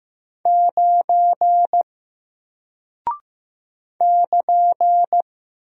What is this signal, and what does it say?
Morse code, heard as a keyed tone, about 15 wpm, 700 Hz. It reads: 9 <KN>
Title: Single Character (Letter:Number:Punctuation:Prosign) 15wpm